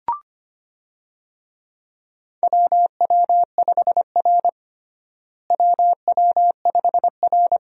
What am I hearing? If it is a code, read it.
Morse code, heard as a keyed tone, about 25 wpm, 700 Hz. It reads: WW5R WW5R
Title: USA Call Signs - 4 Characters - Repeated 2x - 25wpm